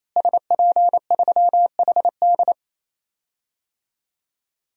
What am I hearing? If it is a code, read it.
Morse code, heard as a keyed tone, about 28 wpm, 700 Hz. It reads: SP3HD
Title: DXCC-100 Call Signs - 5 Characters - Repeated 3x - 28wpm